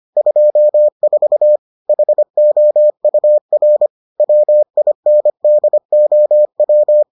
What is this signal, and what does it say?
Morse code, heard as a keyed tone, about 25 wpm, 600 Hz. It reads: 24 HOUR WINDOW